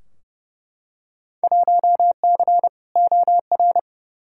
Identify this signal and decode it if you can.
Morse code, heard as a keyed tone, about 30 wpm, 700 Hz. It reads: 1C OR
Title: Field Day Exchanges - 30wpm